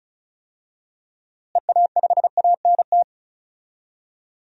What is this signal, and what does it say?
Morse code, heard as a keyed tone, about 36 wpm, 700 Hz. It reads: EA5ANT
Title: DXCC-100 Call Signs - 6 Characters - Repeated 2x - 35wpm